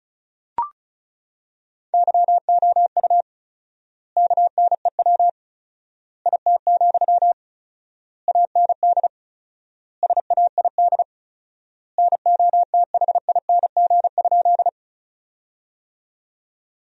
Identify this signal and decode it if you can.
Morse code, heard as a keyed tone, about 34 wpm, 700 Hz. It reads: YOU KNEW IT, AND SAID NOTHING?
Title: 4x Word-Spacing - Sentences from Top 500 Words - 35wpm